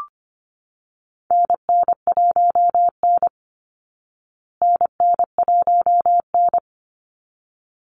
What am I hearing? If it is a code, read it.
Morse code, heard as a keyed tone, about 25 wpm, 700 Hz. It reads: NN1N NN1N
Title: Call Signs Repeated 3x - 25wpm